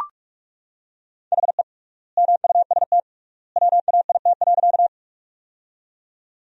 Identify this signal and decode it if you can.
Morse code, heard as a keyed tone, about 41 wpm, 700 Hz. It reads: HE MUST WAIT.
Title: 3x Word-Spacing - Sentences from Top 500 Words - 45wpm